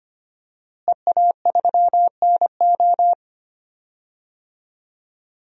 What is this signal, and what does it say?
Morse code, heard as a keyed tone, about 25 wpm, 700 Hz. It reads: EA3NO
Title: DXCC-100 Call Signs - 5 Characters - Repeated 2x - 25wpm